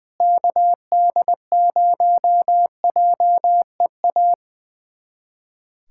Morse code, heard as a keyed tone, about 20 wpm, 700 Hz. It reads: KD0JEA
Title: USA Callsign DE Callsign - 6 Characters - 20wpm